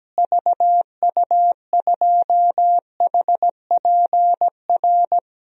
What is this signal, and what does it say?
Morse code, heard as a keyed tone, about 17 wpm, 700 Hz. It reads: VU2HPR
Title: DXCC-100 Call Signs - 6 Characters - Repeated 2x - 17wpm